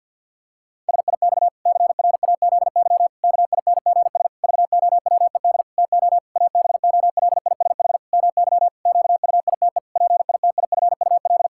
Morse code, heard as a keyed tone, about 51 wpm, 700 Hz. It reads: SIX CRAZY KINGS VOWED TO ABOLISH MY QUITE PITIFUL
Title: Pangram Challenge v2 - 50wpm